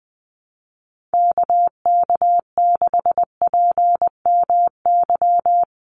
Morse code, heard as a keyed tone, about 20 wpm, 700 Hz. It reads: KK6PMY
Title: USA Call Signs 20wpm